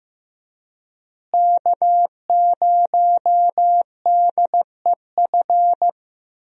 Morse code, heard as a keyed tone, about 14 wpm, 700 Hz. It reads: K0DEF
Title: USA Call Signs - 5 Characters - Repeated 2x - 15wpm